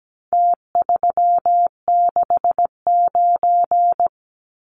Morse code, heard as a keyed tone, about 17 wpm, 700 Hz. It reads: T369
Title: Numbers - Contest Style - 4-Digit 17wpm